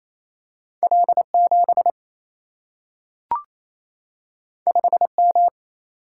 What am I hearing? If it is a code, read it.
Morse code, heard as a keyed tone, about 28 wpm, 700 Hz. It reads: L 7 5 M
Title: Single-Letter Single-Number 28wpm